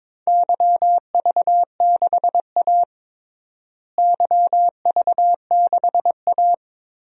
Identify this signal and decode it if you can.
Morse code, heard as a keyed tone, about 22 wpm, 700 Hz. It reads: YV6A YV6A
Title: DXCC-100 Call Signs - 4 Characters - Repeated 2x - 22wpm